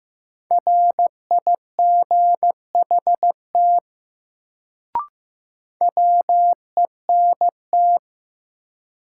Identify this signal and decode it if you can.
Morse code, heard as a keyed tone, about 15 wpm, 700 Hz. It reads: RIGHT WENT
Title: Sentences from Top 200 Words - New Word Drill - 15wpm